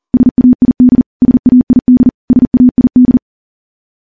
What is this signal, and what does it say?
Morse code, heard as a keyed tone, about 40 wpm, 250 Hz. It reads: SAID SAID SAID
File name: morse / morse-training-40WPM